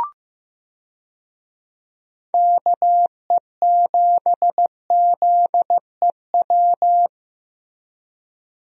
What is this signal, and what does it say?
Morse code, heard as a keyed tone, about 15 wpm, 700 Hz. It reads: KE7ZEW
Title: USA Call Signs - 6 Characters - Repeated 3x - 15wpm